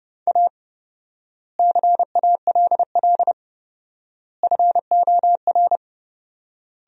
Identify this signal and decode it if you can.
Morse code, heard as a keyed tone, about 30 wpm, 700 Hz. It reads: A CALL FOR
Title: 4x Word-Spacing - Sentences from Top 300 Words - 30wpm